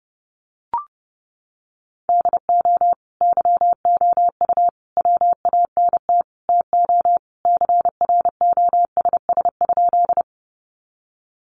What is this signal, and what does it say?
Morse code, heard as a keyed tone, about 30 wpm, 700 Hz. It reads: DO YOU WANT TO CROSS?
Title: Sentences from Top 300 Words 30wpm